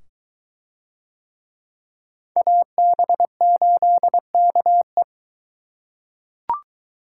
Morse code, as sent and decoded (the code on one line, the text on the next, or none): .- -... ---.. -.- .
AB8KE